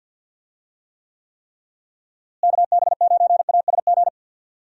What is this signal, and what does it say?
Morse code, heard as a keyed tone, about 51 wpm, 700 Hz. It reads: KB9ASG
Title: USA Call Signs - 6 Characters - Repeated 3x - 50wpm